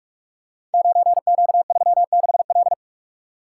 Morse code, heard as a keyed tone, about 44 wpm, 700 Hz. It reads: 9Q36L
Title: Code Groups - 5 Letter:Numbers - 45wpm